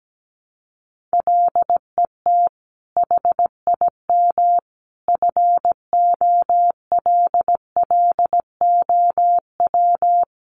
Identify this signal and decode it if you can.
Morse code, heard as a keyed tone, about 17 wpm, 700 Hz. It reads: LET HIM FOLLOW
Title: Sentences from Top 300 Words 17wpm